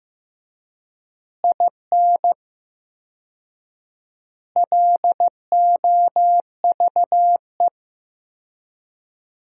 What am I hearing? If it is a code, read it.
Morse code, heard as a keyed tone, about 15 wpm, 700 Hz. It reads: IN LOVE
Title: 4x Word-Spacing - Sentences from Top 400 Words - 15wpm